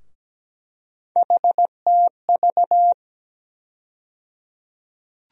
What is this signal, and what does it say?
Morse code, heard as a keyed tone, about 17 wpm, 700 Hz. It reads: HTV
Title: USA Call Sign Postfixes - 3 Letters 17wpm